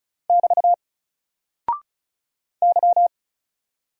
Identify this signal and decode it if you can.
Morse code, heard as a keyed tone, about 36 wpm, 700 Hz. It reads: <BT> Y
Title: Single Character (Letter:Number:Punctuation:Prosign) 35wpm